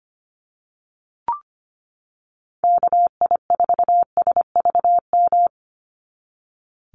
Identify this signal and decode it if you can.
Morse code, heard as a keyed tone, about 25 wpm, 700 Hz. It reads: KI4SVM
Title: NA SOTA Chaser Callsigns 25wpm